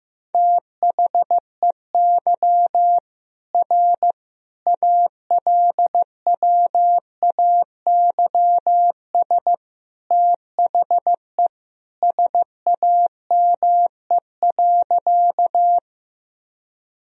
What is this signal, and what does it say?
Morse code, heard as a keyed tone, about 16 wpm, 700 Hz. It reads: THEY R ALWAYS THE SAME.
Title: Sentences from Top 300 Words and Abbreviations 15wpm